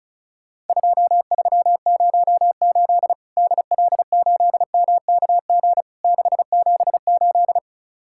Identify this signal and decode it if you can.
Morse code, heard as a keyed tone, about 36 wpm, 700 Hz. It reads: 2308 DL8MKG 678
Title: Worked All Europe DX Contest - QTC Exchange - 35wpm